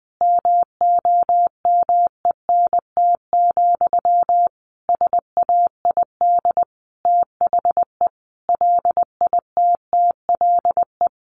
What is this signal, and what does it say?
Morse code, heard as a keyed tone, about 20 wpm, 700 Hz. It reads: MOMENT, SAID THE LITTLE
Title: Sentences from Top 1000 Words 20wpm